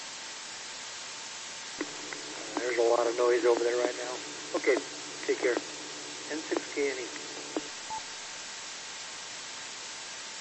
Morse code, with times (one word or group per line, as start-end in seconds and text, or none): none